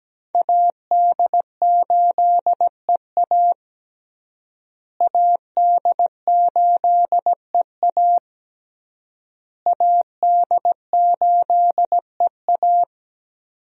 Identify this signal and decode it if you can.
Morse code, heard as a keyed tone, about 17 wpm, 700 Hz. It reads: AD8EA AD8EA AD8EA
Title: USA Call Signs - 5 Characters - Repeated 3x - 17wpm